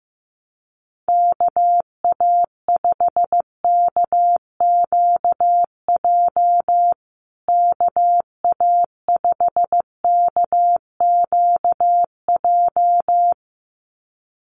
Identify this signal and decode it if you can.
Morse code, heard as a keyed tone, about 15 wpm, 700 Hz. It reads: KA5KQJ KA5KQJ
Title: USA Call Signs Repeated 2x 15wpm